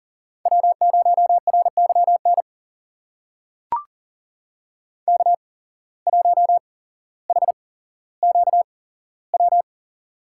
Morse code, heard as a keyed tone, about 42 wpm, spaced as for 17 wpm, 700 Hz. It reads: W0RYN K1HQW
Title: USA Call Signs - 5 Characters - 8x Character Spacing - 40wpm